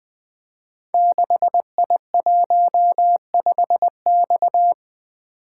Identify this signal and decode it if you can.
Morse code, heard as a keyed tone, about 20 wpm, 700 Hz. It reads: 6I15X
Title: Code Groups - 5 Letter:Numbers - 20wpm